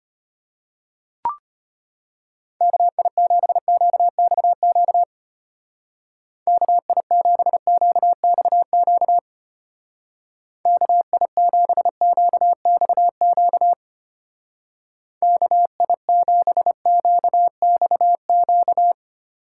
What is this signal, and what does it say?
Morse code, heard as a keyed tone, about 31 wpm, 700 Hz. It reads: KI7QXQ KI7QXQ KI7QXQ KI7QXQ
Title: Speed Racing - USA 2x3 Call Signs 25wpm